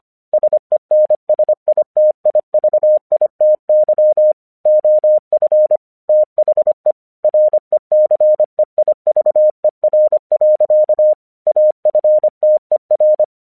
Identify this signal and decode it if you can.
Morse code, heard as a keyed tone, about 25 wpm, 610 Hz. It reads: SENSITIVITY OF THE RECEIVER. AFTER
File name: 1 English 25wpm / AQC29